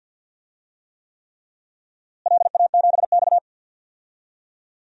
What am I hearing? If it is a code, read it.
Morse code, heard as a keyed tone, about 51 wpm, 700 Hz. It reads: LA7X
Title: DXCC-100 Call Signs - 4 Characters - Repeated 3x - 50wpm